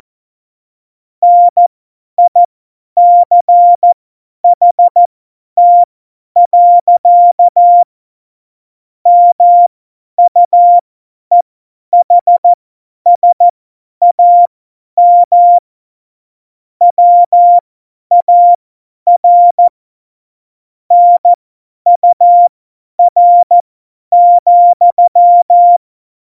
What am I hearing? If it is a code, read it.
Morse code, heard as a keyed tone, about 14 wpm, spaced as for 10 wpm, 700 Hz. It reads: NICHT. MUEHSAM WAR NUR,
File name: Morsekurs / Witze / Witz 11 Juli 17, Tempo 50 F70